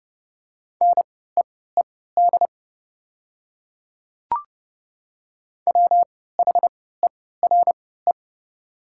Morse code, heard as a keyed tone, about 29 wpm, spaced as for 17 wpm, 690 Hz. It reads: NEED WHERE
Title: 3x Character Spacing - Top 200 Words - 30wpm